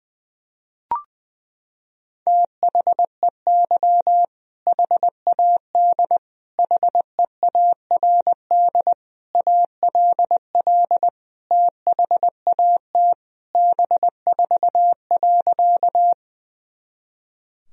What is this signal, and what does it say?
Morse code, heard as a keyed tone, about 21 wpm, 700 Hz. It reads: THEY HAD HEARD ALL THAT B4.
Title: Sentences from Top 400 Words and Abbreviations 20wpm